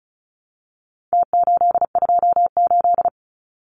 Morse code, heard as a keyed tone, about 34 wpm, 700 Hz. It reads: T828
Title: Numbers - Contest Style - 4-Digit 35wpm